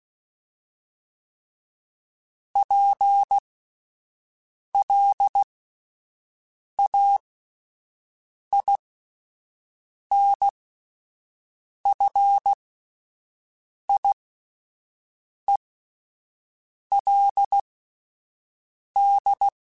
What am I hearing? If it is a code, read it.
Morse code, heard as a keyed tone, about 16 wpm, spaced as for 5 wpm, 800 Hz. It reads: PLAINFIELD